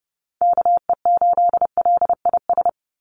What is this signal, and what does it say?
Morse code, heard as a keyed tone, about 30 wpm, 700 Hz. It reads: KE8LIS
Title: USA Call Signs 30wpm